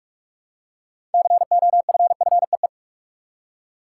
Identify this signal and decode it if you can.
Morse code, heard as a keyed tone, about 46 wpm, 700 Hz. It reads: COFFEE
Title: Six Letter Words - 45wpm